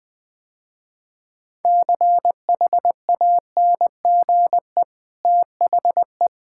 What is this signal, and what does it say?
Morse code, heard as a keyed tone, about 20 wpm, 700 Hz. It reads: CHANGE THE
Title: Easy Sentences with Hints - 20wpm